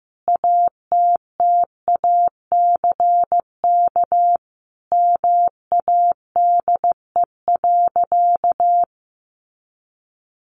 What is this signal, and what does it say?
Morse code, heard as a keyed tone, about 15 wpm, 700 Hz. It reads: ATTACK MADE.
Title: Sentences from Top 1500 Words 15wpm